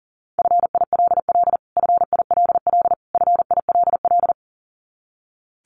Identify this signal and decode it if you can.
Morse code, heard as a keyed tone, about 40 wpm, 700 Hz. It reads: FILL FILL FILL
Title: Top 500 Words Repeated 40wpm